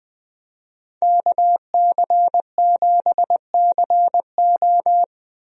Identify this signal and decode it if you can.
Morse code, heard as a keyed tone, about 20 wpm, 700 Hz. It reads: KC7CO
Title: USA Call Signs - 5 Characters - Repeated 2x - 20wpm